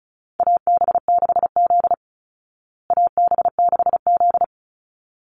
Morse code, heard as a keyed tone, about 34 wpm, 700 Hz. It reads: AB6Z AB6Z
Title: Call Signs Repeated 3x - 35wpm